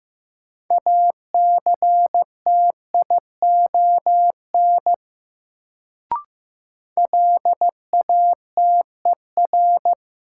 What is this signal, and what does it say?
Morse code, heard as a keyed tone, about 15 wpm, 700 Hz. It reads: ACTION LATER
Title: Sentences from Top 700 Words - New Word Drill - 15wpm